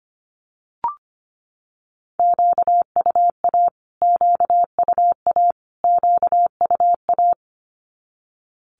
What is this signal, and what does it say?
Morse code, heard as a keyed tone, about 25 wpm, 700 Hz. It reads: QUA QUA QUA
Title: Q-Codes Repeated 3x 25wpm